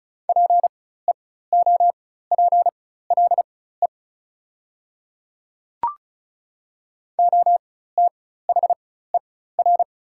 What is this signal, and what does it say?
Morse code, heard as a keyed tone, about 34 wpm, spaced as for 16 wpm, 700 Hz. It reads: PEOPLE OTHER
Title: 4x Character Spacing - Top 100 Words - 35wpm